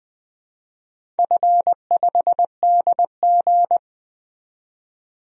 Morse code, heard as a keyed tone, about 20 wpm, 700 Hz. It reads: F5DG
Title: DXCC-100 Call Signs - 4 Characters - 20wpm